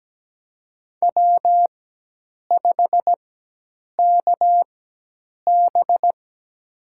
Morse code, heard as a keyed tone, about 17 wpm, spaced as for 8 wpm, 700 Hz. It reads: W5KB
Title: USA Call Signs - 4 Characters - 4x Character Spacing - 17wpm